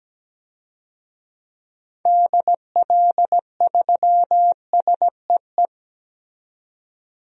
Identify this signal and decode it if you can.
Morse code, heard as a keyed tone, about 17 wpm, 700 Hz. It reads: DL3SEE
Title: DXCC-100 Call Signs - 6 Characters - 17wpm